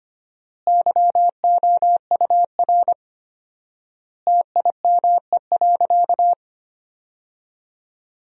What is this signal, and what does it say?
Morse code, heard as a keyed tone, about 25 wpm, 700 Hz. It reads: YOUR TIME.
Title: 4x Word-Spacing - Sentences from Top 100 Words - 25wpm